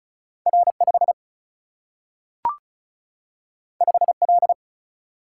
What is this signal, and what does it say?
Morse code, heard as a keyed tone, about 36 wpm, 700 Hz. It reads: R5 5L
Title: Single-Letter Single-Number 35wpm